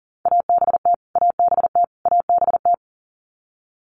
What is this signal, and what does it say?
Morse code, heard as a keyed tone, about 40 wpm, 700 Hz. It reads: ABT ABT ABT
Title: Abbreviations Repeated 3x - 40wpm